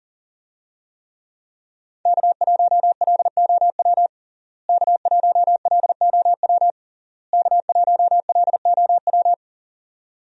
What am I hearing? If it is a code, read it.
Morse code, heard as a keyed tone, about 40 wpm, 700 Hz. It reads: K1LOW K1LOW K1LOW
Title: USA Call Signs - 5 Characters - Repeated 3x - 40wpm